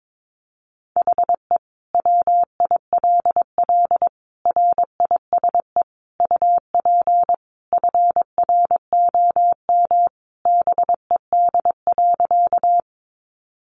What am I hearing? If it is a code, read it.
Morse code, heard as a keyed tone, about 22 wpm, 700 Hz. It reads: HE WILL RISE UP FROM BED.
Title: Sentences from Top 900 Words 22wpm